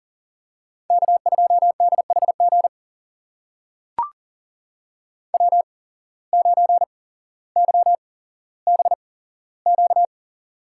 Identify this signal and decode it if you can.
Morse code, heard as a keyed tone, about 39 wpm, spaced as for 17 wpm, 700 Hz. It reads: K2DHG W9YBQ
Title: USA Call Signs - 5 Characters - 8x Character Spacing - 40wpm